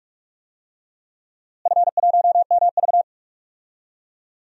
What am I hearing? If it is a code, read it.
Morse code, heard as a keyed tone, about 41 wpm, 700 Hz. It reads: F1MV
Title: DXCC-100 Call Signs - 4 Characters - Repeated 2x - 45wpm